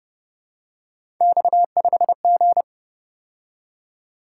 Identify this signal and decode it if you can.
Morse code, heard as a keyed tone, about 32 wpm, 700 Hz. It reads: X5G